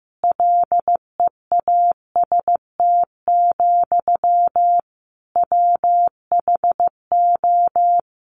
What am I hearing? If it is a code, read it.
Morse code, heard as a keyed tone, about 14 wpm, 700 Hz. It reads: LEAST, WHO
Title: Sentences from Top 700 Words 15wpm